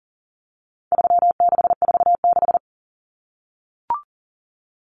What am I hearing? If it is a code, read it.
Morse code, heard as a keyed tone, about 40 wpm, 700 Hz. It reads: 3646
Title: Common Cut Numbers - 4-Digit 40wpm